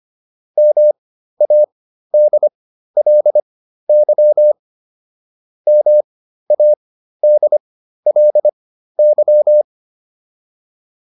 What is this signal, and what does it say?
Morse code, heard as a keyed tone, about 25 wpm, spaced as for 13 wpm, 600 Hz. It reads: MADLY MADLY